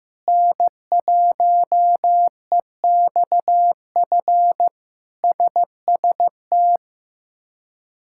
Morse code, heard as a keyed tone, about 15 wpm, 700 Hz. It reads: N1EXF SST
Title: K1USN SST QSO Practice - 15wpm